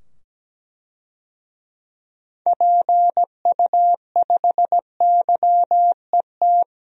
Morse code, heard as a keyed tone, about 17 wpm, 700 Hz. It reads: PU5YET
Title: DXCC-100 Call Signs - 6 Characters - 17wpm